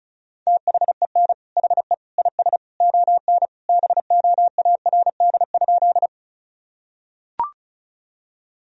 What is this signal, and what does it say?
Morse code, heard as a keyed tone, about 34 wpm, 700 Hz. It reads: THEN HE IS ON BOARD?
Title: Sentences from Top 1000 Words - Review Part 4 - 35wpm